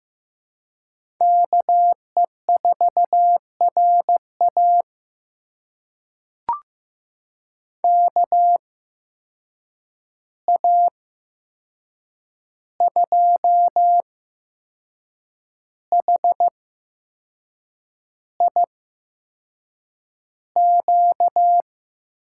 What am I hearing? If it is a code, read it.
Morse code, heard as a keyed tone, about 15 wpm, spaced as for 6 wpm, 700 Hz. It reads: KE4RA KA2HIQ
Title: USA Call Signs - 8x Character Spacing - 15wpm